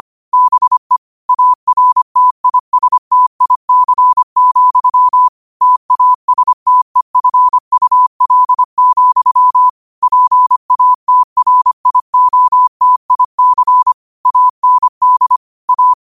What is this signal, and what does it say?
Morse code, heard as a keyed tone, about 25 wpm, 1000 Hz. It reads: BE ARTISTIC, TASTEFUL, PATRIOTIC AND A
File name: qotdfiles / 25wpm